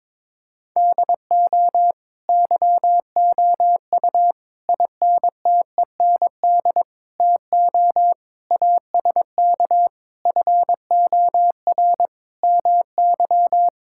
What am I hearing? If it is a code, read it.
Morse code, heard as a keyed tone, about 22 wpm, 700 Hz. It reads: DO YOU INTEND TO ASK FOR MY
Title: Sentences from Top 1600 Words 22wpm